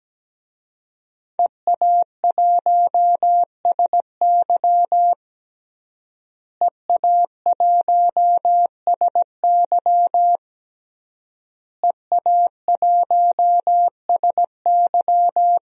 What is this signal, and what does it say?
Morse code, heard as a keyed tone, about 17 wpm, 700 Hz. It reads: EA1SY EA1SY EA1SY
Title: DXCC-100 Call Signs - 5 Characters - Repeated 3x - 17wpm